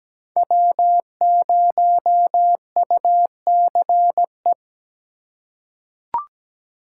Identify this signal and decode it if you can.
Morse code, heard as a keyed tone, about 17 wpm, 700 Hz. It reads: W0UCE
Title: K1USN SST QSO Practice - 17wpm